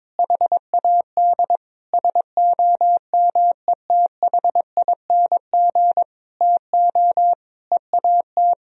Morse code, heard as a keyed tone, about 22 wpm, 700 Hz. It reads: HAD SOMETHING TO EAT
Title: Sentences from Top 1000 Words - Review Part 4 - 22wpm